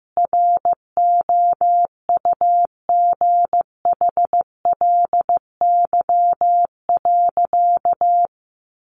Morse code, heard as a keyed tone, about 15 wpm, 700 Hz. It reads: ROUGHLY.
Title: Sentences from Top 1500 Words 15wpm